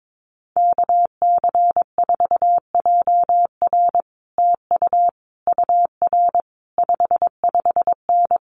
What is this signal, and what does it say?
Morse code, heard as a keyed tone, about 22 wpm, 700 Hz. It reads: KC4JR TU UR 55N
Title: Common QSO Phrases Encore 22wpm